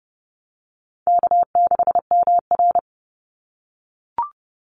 Morse code, heard as a keyed tone, about 30 wpm, 700 Hz. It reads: K6MR
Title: Call Signs - 30wpm